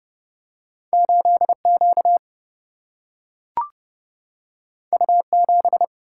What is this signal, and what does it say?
Morse code, heard as a keyed tone, about 30 wpm, 700 Hz. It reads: 8Q U7
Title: Single-Letter Single-Number 30wpm